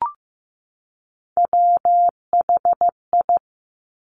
Morse code, heard as a keyed tone, about 15 wpm, 700 Hz. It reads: WHI
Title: Top Three Letter Combinations in Words 15wpm